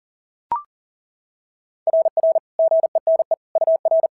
Morse code, heard as a keyed tone, about 40 wpm, 660 Hz. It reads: RR GENE UR
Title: SOTA QSO Practice 40wpm